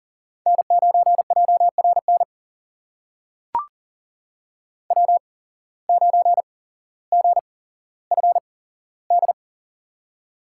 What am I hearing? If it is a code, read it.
Morse code, heard as a keyed tone, about 39 wpm, spaced as for 17 wpm, 700 Hz. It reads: N9JRN W9GFD
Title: USA Call Signs - 5 Characters - 8x Character Spacing - 40wpm